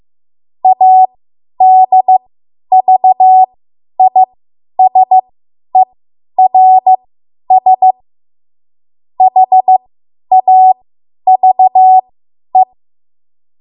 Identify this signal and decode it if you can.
Morse code, heard as a keyed tone, about 15 wpm, spaced as for 10 wpm, 750 Hz. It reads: ADVISERS HAVE
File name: Morse / 10WPM